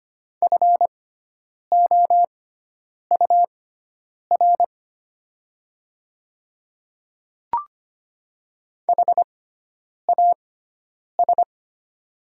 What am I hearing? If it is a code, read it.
Morse code, heard as a keyed tone, about 25 wpm, spaced as for 9 wpm, 700 Hz. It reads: FOUR HAS
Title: 6x Character Spacing - Top 300 Words - 25wpm